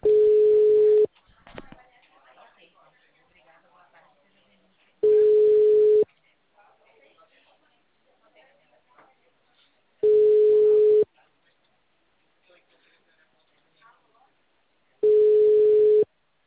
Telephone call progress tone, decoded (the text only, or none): ringback tone